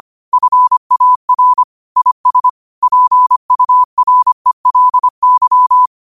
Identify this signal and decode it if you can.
Morse code, heard as a keyed tone, about 25 wpm, 1000 Hz. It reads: FAR IS PURELY
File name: qotdfiles / 25wpm